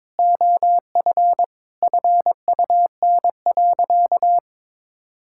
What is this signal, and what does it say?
Morse code, heard as a keyed tone, about 22 wpm, 700 Hz. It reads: OF FUN.
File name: practice / 22wpm / Sentences with 4-Word Phrases - 22wpm